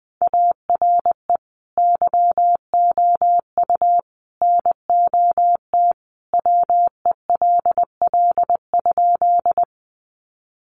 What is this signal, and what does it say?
Morse code, heard as a keyed tone, about 19 wpm, 700 Hz. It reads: ARE YOU NOT WELL?
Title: Sentences from Top 600 Words 20wpm